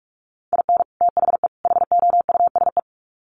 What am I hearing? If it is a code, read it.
Morse code, heard as a keyed tone, about 44 wpm, 700 Hz. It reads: IN THE HOUSE
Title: Sets of 3 Words 45wpm